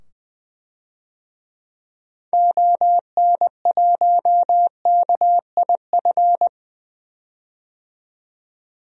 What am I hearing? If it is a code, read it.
Morse code, heard as a keyed tone, about 20 wpm, 700 Hz. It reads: ON1KIF
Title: DXCC-100 Call Signs - 6 Characters - 20wpm